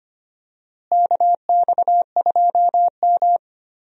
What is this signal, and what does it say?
Morse code, heard as a keyed tone, about 25 wpm, 700 Hz. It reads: KX2M
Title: USA Call Signs - 4 Characters - Repeated 2x - 25wpm